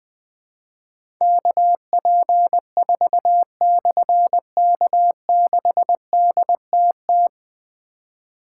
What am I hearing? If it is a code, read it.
Morse code, heard as a keyed tone, about 20 wpm, 700 Hz. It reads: KP4/K6DTT
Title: Difficult Call Signs 20wpm